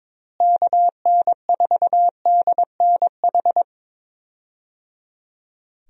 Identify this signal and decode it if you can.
Morse code, heard as a keyed tone, about 22 wpm, 700 Hz. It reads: KN4DNH
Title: USA Call Signs - 6 Characters - Repeated 3x - 22wpm